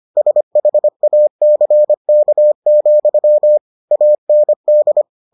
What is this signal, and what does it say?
Morse code, heard as a keyed tone, about 25 wpm, 600 Hz. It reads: SHACK, AND